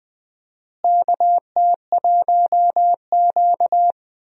Morse code, heard as a keyed tone, about 20 wpm, 700 Hz. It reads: KT1Q
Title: USA Call Signs - 4 Characters - Repeated 3x - 20wpm